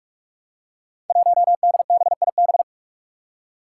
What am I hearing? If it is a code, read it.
Morse code, heard as a keyed tone, about 46 wpm, 700 Hz. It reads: 1DBIB
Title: Five Challenging Letter-Number Combinations 45wpm